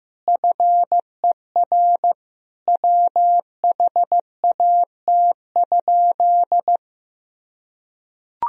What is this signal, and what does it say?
Morse code, heard as a keyed tone, about 15 wpm, 700 Hz. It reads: FER WHAT?
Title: Sentences from Top 100 Words and Abbreviations 15wpm